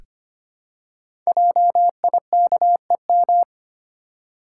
Morse code, heard as a keyed tone, about 25 wpm, 700 Hz. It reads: JIKEM
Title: Code Groups - 5 Letters - 25wpm